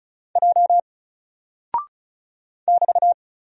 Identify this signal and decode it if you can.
Morse code, heard as a keyed tone, about 36 wpm, 700 Hz. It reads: J <BT>
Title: Single Character (Letter:Number:Punctuation:Prosign) Alt 35wpm